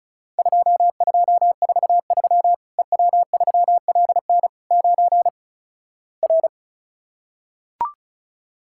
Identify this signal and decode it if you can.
Morse code, heard as a keyed tone, about 36 wpm, 700 Hz. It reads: 2243 EW3LN 9 R
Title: Worked All Europe DX Contest - QTC Exchange - 35wpm